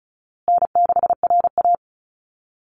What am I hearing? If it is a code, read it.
Morse code, heard as a keyed tone, about 34 wpm, 700 Hz. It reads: N6RA
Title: Easy Call Signs Repeated 3x - 35wpm